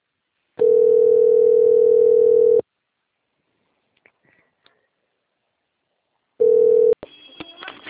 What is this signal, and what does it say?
Ringback tone (North America), 440 + 480 Hz.
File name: rg-1600-unknown- - - .40967